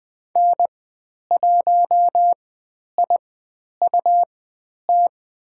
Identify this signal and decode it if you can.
Morse code, heard as a keyed tone, about 20 wpm, spaced as for 10 wpm, 700 Hz. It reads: N1IUT
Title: 109 Lesson 18g - Letter m - Call Signs 20wpm